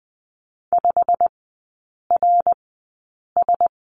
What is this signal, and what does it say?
Morse code, heard as a keyed tone, about 20 wpm, spaced as for 8 wpm, 700 Hz. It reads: 5RS
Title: 093 Lesson 16e - Letter c - Three Characters 20wpm